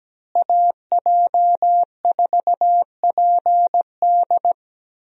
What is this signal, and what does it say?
Morse code, heard as a keyed tone, about 17 wpm, 700 Hz. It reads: AJ4PD
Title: USA Call Signs - 5 Characters - Repeated 2x - 17wpm